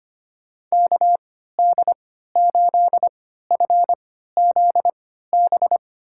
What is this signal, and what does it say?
Morse code, heard as a keyed tone, about 25 wpm, spaced as for 14 wpm, 700 Hz. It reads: KD8FZB
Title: USA Call Signs - 6 Characters - 3x Character Spacing - 25wpm